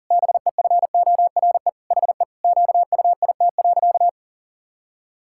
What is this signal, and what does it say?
Morse code, heard as a keyed tone, about 37 wpm, 700 Hz. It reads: BEFORE HE QUIT.
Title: Pangram Challenge v2 - 40wpm